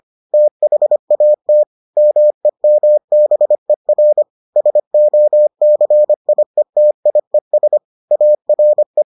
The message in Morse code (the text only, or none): THAT MEMBER SOCIETIES ARE